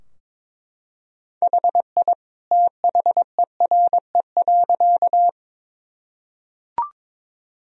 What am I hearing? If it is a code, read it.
Morse code, heard as a keyed tone, about 22 wpm, 700 Hz. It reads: HI THERE.
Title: Sentences from Top 1000 Words - Review Part 3 - 22wpm